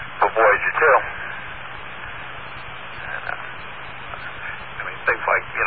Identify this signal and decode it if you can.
Emergency Alert System (SAME) tones, heard as harsh buzzing bursts, with speech. It is heard from 1.1 to 5.2 s.